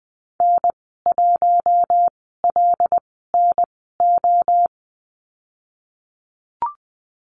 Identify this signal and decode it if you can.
Morse code, heard as a keyed tone, about 20 wpm, spaced as for 15 wpm, 700 Hz. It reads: N1LNO
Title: 088 Lesson 15g - Number 5 - Call Signs 20wpm